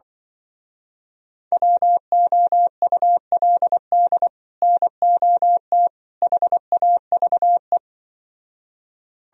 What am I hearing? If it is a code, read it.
Morse code, heard as a keyed tone, about 23 wpm, 700 Hz. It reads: WOULD NOT HAVE
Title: Speed-Racing - Sets of 3 Words 20wpm